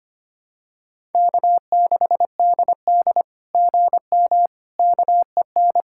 Morse code, heard as a keyed tone, about 24 wpm, 700 Hz. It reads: K6DD GM KEN
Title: SOTA QSO Practice 25wpm